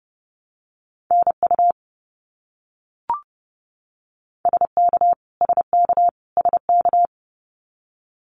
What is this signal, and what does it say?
Morse code, heard as a keyed tone, about 30 wpm, 700 Hz. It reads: NU SK SK SK
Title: Canadian Provinces and Territories - Repeated 3x - 30wpm